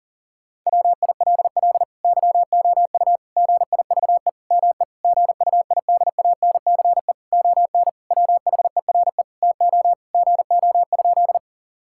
Morse code, heard as a keyed tone, about 40 wpm, 700 Hz. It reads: WILL YOU GIVE ME GUIDANCE ON WHERE TO GO?
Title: Sentences from Top 1600 Words 40wpm